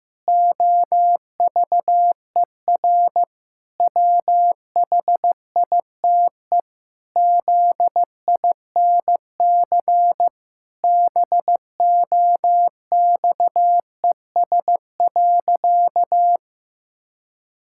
Morse code, heard as a keyed tone, about 15 wpm, 700 Hz. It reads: OVER WHITE ZINC BOXES.
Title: Pangram Challenge v2 - 15wpm